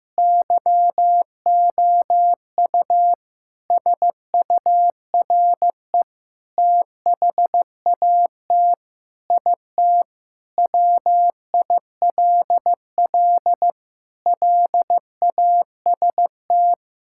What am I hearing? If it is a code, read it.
Morse code, heard as a keyed tone, about 15 wpm, 700 Hz. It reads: YOU SURE THAT IT WILL LAST
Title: Sentences from Top 1600 Words 15wpm